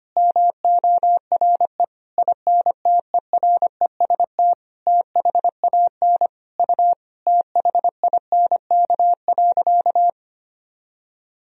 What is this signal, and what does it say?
Morse code, heard as a keyed tone, about 25 wpm, 700 Hz. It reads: MORE INTEREST THAN U THINK.
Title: Sentences from Top 400 Words and Abbreviations 25wpm